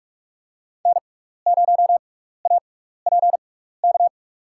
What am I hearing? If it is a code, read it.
Morse code, heard as a keyed tone, about 46 wpm, spaced as for 15 wpm, 700 Hz. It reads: N0APK
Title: USA Call Signs - 6x Character Spacing - 45wpm